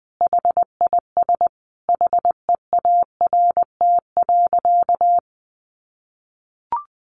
Morse code, heard as a keyed tone, about 20 wpm, 700 Hz. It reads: HIS HEART.
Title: Sentences from Top 1200 Words 20wpm